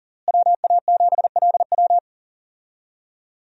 Morse code, heard as a keyed tone, about 40 wpm, 700 Hz. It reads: WA7LW
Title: USA Call Signs - 5 Characters - Repeated 3x - 40wpm